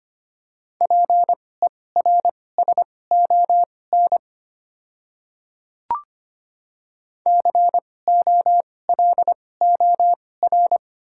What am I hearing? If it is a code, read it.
Morse code, heard as a keyed tone, about 25 wpm, spaced as for 18 wpm, 700 Hz. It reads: PERSON COLOR
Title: 2x Character Spacing - Top 500 Words - 25wpm